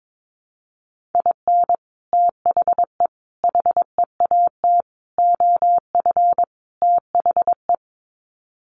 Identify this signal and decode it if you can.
Morse code, heard as a keyed tone, about 22 wpm, 700 Hz. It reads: IN THE HEAT OF THE
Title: Sets of 5 Words Encore2 22wpm